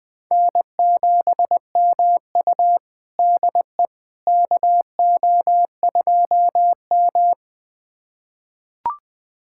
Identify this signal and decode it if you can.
Morse code, heard as a keyed tone, about 20 wpm, 700 Hz. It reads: N7MU DE KO2M
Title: USA Callsign DE Callsign - 4 Characters - 20wpm